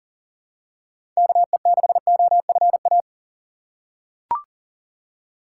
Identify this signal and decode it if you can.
Morse code, heard as a keyed tone, about 40 wpm, 700 Hz. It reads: KE6OFA
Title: USA Call Signs - 3x Character Spacing - 40wpm